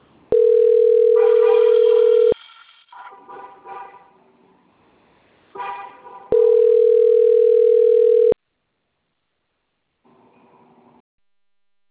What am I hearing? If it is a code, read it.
Ringback tone (North America), 440 + 480 Hz.